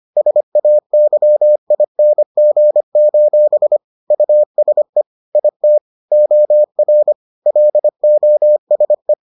Morse code, heard as a keyed tone, about 25 wpm, 600 Hz. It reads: SAYING: USE IT OR LOSE